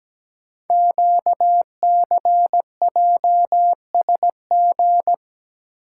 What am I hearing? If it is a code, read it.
Morse code, heard as a keyed tone, about 17 wpm, 700 Hz. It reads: QCJSG
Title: Code Groups - 5 Letters - 17wpm